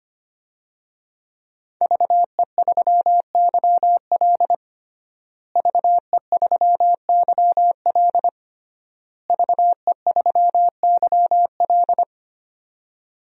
Morse code, heard as a keyed tone, about 25 wpm, 700 Hz. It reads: VE3YL VE3YL VE3YL
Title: DXCC-100 Call Signs - 5 Characters - Repeated 3x - 25wpm